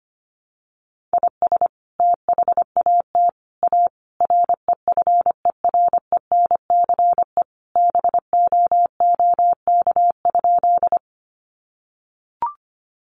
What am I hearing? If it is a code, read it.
Morse code, heard as a keyed tone, about 25 wpm, 700 Hz. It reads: IS THAT A REFERENCE BOOK?
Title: Sentences from Top 1200 Words 25wpm